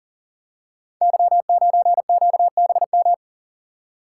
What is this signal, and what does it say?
Morse code, heard as a keyed tone, about 40 wpm, 690 Hz. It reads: Y9QBM
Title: Code Groups - 5 Letter:Numbers - 40wpm